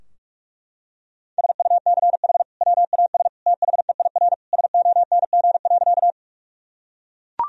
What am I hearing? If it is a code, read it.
Morse code, heard as a keyed tone, about 44 wpm, 700 Hz. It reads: SUCH WAS THEIR SONG.
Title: Sentences from Top 1000 Words - Review Part 2 - 45wpm